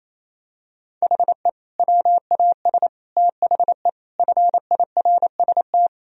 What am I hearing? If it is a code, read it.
Morse code, heard as a keyed tone, about 28 wpm, 700 Hz. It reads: HE WAS THE FIRST
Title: Sentences from Top 1000 Words - Review Part 1 - 28wpm